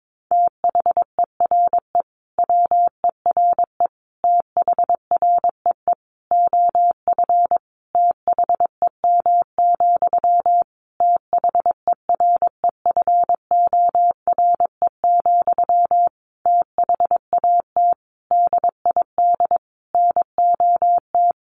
Morse code, heard as a keyed tone, about 22 wpm, 700 Hz. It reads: THERE WERE THREE OF THEM, THEREFORE, THAT DID NOT
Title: Sentences from Top 800 Words 22wpm